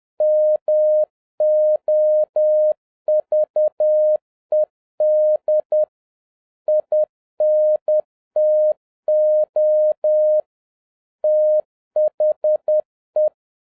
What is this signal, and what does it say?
Morse code, heard as a keyed tone, about 10 wpm, 610 Hz. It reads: MOVED INTO THE
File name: 2011-07-mp3 / n41 EXCITING QRP CW OPERATION 10wpm